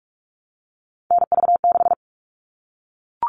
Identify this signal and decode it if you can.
Morse code, heard as a keyed tone, about 44 wpm, 700 Hz. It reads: NV6
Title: All Cut Numbers - 3-Digit 45wpm